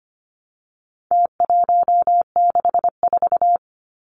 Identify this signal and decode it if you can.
Morse code, heard as a keyed tone, about 25 wpm, 700 Hz. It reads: T164
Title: Numbers - Contest Style - 4-Digit 25wpm